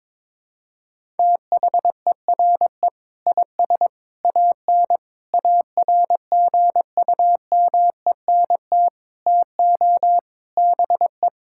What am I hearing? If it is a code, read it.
Morse code, heard as a keyed tone, about 22 wpm, 700 Hz. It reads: THERE IS AN ARGUMENT TO BE
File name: practice / 22wpm / Sets of 8 Words - 22wpm